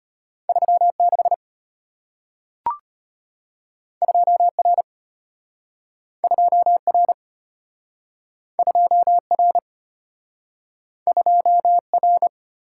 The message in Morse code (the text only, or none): 36 2R 2R 2R 2R